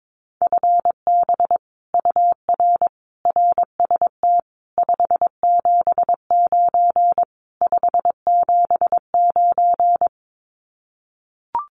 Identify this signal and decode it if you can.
Morse code, heard as a keyed tone, about 22 wpm, 700 Hz. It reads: FB UR RST 579 579
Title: Common QSO Phrases Encore 22wpm